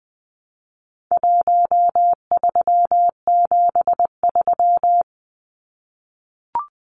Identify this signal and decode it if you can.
Morse code, heard as a keyed tone, about 20 wpm, 700 Hz. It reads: 1373
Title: Numbers - Contest Style - 4-Digit 20wpm